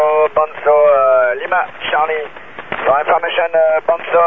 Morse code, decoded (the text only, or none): none